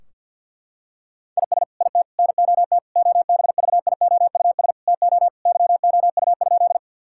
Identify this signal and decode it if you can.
Morse code, heard as a keyed tone, about 50 wpm, 700 Hz. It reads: IS IT NOT OBVIOUS TO YOU?
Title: Sentences from Top 1600 Words 50wpm